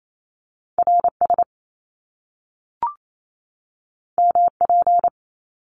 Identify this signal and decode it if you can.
Morse code, heard as a keyed tone, about 28 wpm, 700 Hz. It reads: RS MP
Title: Top Two Letter Combinations in Words 28wpm